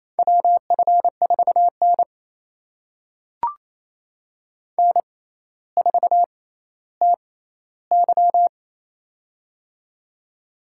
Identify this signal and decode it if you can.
Morse code, heard as a keyed tone, about 28 wpm, spaced as for 14 wpm, 700 Hz. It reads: WF4N N4TY
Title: USA Call Signs - 4 Characters - 6x Character Spacing - 28wpm